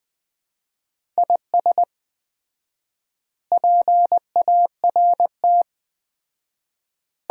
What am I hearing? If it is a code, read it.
Morse code, heard as a keyed tone, about 21 wpm, 700 Hz. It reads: IS PART
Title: 4x Word-Spacing - Sets of 4 Words - 20wpm